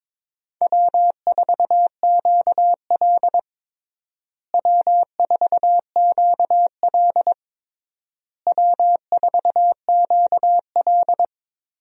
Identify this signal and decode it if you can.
Morse code, heard as a keyed tone, about 22 wpm, 700 Hz. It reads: W4QL W4QL W4QL
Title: USA Call Signs - 4 Characters - Repeated 3x - 22wpm